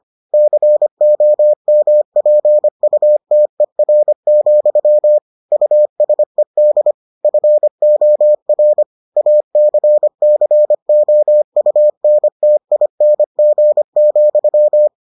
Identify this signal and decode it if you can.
Morse code, heard as a keyed tone, about 25 wpm, 600 Hz. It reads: COMPUTER, USED FOR ACCOUNTING,